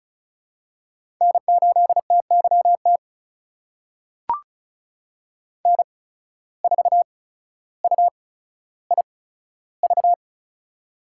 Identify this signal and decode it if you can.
Morse code, heard as a keyed tone, about 36 wpm, spaced as for 15 wpm, 700 Hz. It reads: N8TYT N4UIV
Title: USA Call Signs - 5 Characters - 8x Character Spacing - 35wpm